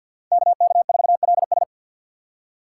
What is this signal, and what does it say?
Morse code, heard as a keyed tone, about 45 wpm, 700 Hz. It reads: KK4LS
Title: USA Call Signs - 5 Characters - 2x Character Spacing - 50wpm